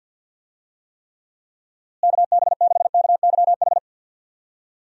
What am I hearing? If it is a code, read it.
Morse code, heard as a keyed tone, about 52 wpm, 700 Hz. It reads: KB6KYH
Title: USA Call Signs - 6 Characters - Repeated 2x - 50wpm